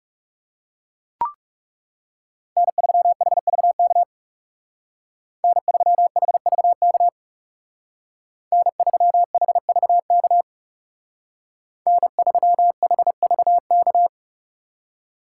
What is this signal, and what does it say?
Morse code, heard as a keyed tone, about 39 wpm, 700 Hz. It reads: N3HVK N3HVK N3HVK N3HVK